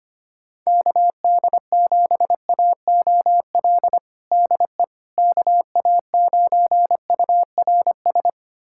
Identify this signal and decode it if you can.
Morse code, heard as a keyed tone, about 25 wpm, 700 Hz. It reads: KD7AOL DE KA9URS
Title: USA Callsign DE Callsign 25wpm